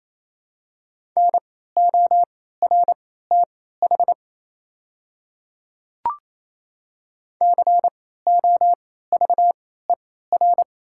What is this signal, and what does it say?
Morse code, heard as a keyed tone, about 28 wpm, spaced as for 16 wpm, 700 Hz. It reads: NORTH COVER
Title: 3x Character Spacing - Top 300 Words - 28wpm